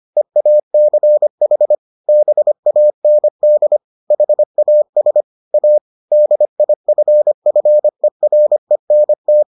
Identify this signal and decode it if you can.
Morse code, heard as a keyed tone, about 25 wpm, 600 Hz. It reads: EACH BAND HAS A DIFFERENT